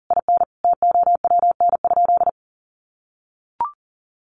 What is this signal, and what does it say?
Morse code, heard as a keyed tone, about 40 wpm, 700 Hz. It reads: IN TOWN?
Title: Sentences from Top 1200 Words 40wpm